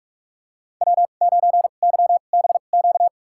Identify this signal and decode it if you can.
Morse code, heard as a keyed tone, about 46 wpm, spaced as for 32 wpm, 700 Hz. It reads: W9YBQ
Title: USA Call Signs - 5 Characters - 2x Character Spacing - 45wpm